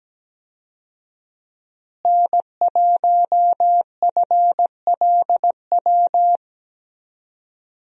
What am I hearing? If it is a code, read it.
Morse code, heard as a keyed tone, about 17 wpm, 700 Hz. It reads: N1FLW
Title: USA Call Signs - 5 Characters - Repeated 3x - 17wpm